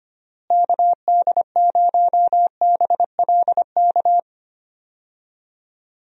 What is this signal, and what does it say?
Morse code, heard as a keyed tone, about 25 wpm, 700 Hz. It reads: KD0BLK
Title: USA Call Signs - 6 Characters - Repeated 3x - 25wpm